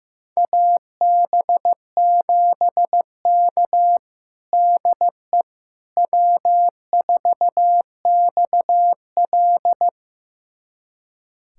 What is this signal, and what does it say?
Morse code, heard as a keyed tone, about 15 wpm, 700 Hz. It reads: AB7K DE W4XL
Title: USA Callsign DE Callsign - 4 Characters - 15wpm